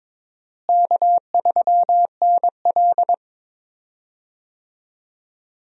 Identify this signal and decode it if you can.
Morse code, heard as a keyed tone, about 21 wpm, 700 Hz. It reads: K3NL
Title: USA Call Signs - 4 Characters - Repeated 3x - 22wpm